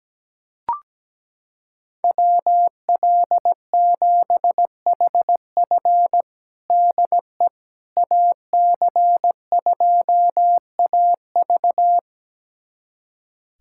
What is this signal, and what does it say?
Morse code, heard as a keyed tone, about 17 wpm, 700 Hz. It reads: WL7HF DE AC2AV
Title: USA Callsign DE Callsign - 5 Characters - 17wpm